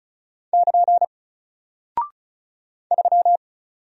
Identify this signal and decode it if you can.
Morse code, heard as a keyed tone, about 36 wpm, 700 Hz. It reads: <KN> 3
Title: Single Character (Letter:Number:Punctuation:Prosign) Alt 35wpm